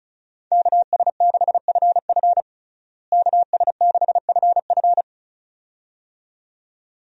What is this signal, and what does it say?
Morse code, heard as a keyed tone, about 36 wpm, 700 Hz. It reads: KS6FF KS6FF
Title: USA Call Signs - 5 Characters - Repeated 2x - 35wpm